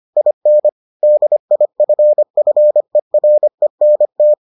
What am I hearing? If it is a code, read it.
Morse code, heard as a keyed tone, about 25 wpm, 600 Hz. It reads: IN DIFFERENT